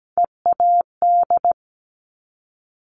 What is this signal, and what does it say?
Morse code, heard as a keyed tone, about 17 wpm, 700 Hz. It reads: EAD
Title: Top Three Letter Combinations in Words 17wpm